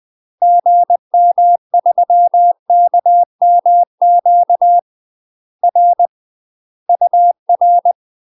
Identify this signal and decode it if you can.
Morse code, heard as a keyed tone, about 20 wpm, 700 Hz. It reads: GM3KMQ R UR
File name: UK MASTER QSOs TRAINING / 20WPM / UK MASTER QSOs 0052